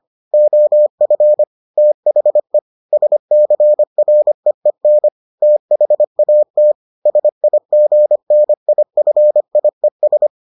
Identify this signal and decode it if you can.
Morse code, heard as a keyed tone, about 25 wpm, 600 Hz. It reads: OF THE SCREEN THAT SIGNIFIES